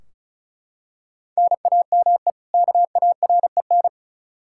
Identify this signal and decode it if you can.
Morse code, heard as a keyed tone, about 36 wpm, 700 Hz. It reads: NAME KAREN
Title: Practice QSO - Exchange 1 - Beginning - 35wpm